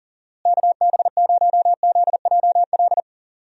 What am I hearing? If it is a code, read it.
Morse code, heard as a keyed tone, about 38 wpm, 700 Hz. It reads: KB0ZJL
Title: USA Call Signs - 6 Characters - Repeated 2x - 40wpm